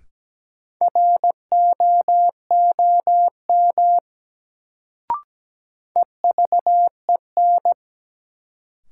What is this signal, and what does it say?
Morse code, heard as a keyed tone, about 17 wpm, 700 Hz. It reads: ROOM EVEN
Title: Sets of 4 Words - New Word Drill - 17wpm